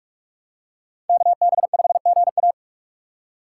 Morse code, heard as a keyed tone, about 46 wpm, 700 Hz. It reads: KB5GA
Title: K1USN SST QSO Practice - 45wpm